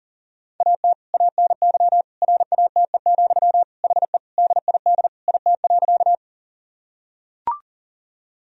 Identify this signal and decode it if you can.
Morse code, heard as a keyed tone, about 40 wpm, 690 Hz. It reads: AT ANY RATE, HE DID IT.
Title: Sentences from Top 1000 Words - Review Part 4 - 40wpm